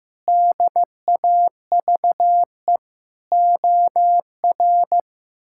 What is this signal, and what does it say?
Morse code, heard as a keyed tone, about 15 wpm, 700 Hz. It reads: DAVE OR
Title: K1USN SST QSO Practice - 15wpm